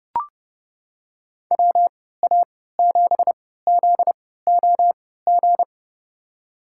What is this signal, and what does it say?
Morse code, heard as a keyed tone, about 31 wpm, spaced as for 17 wpm, 700 Hz. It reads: WA7ZOG
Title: USA Call Signs - 6 Characters - 3x Character Spacing - 30wpm